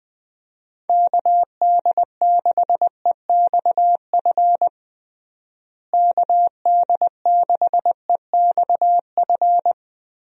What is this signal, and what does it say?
Morse code, heard as a keyed tone, about 20 wpm, 700 Hz. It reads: KD6EXF KD6EXF
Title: USA Call Signs - 6 Characters - Repeated 3x - 20wpm